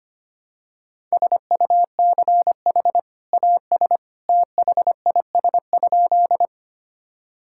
Morse code, heard as a keyed tone, about 25 wpm, 700 Hz. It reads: SUCH AS THIS?
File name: practice / 25wpm / Sentences with 2-Word Phrases - 25wpm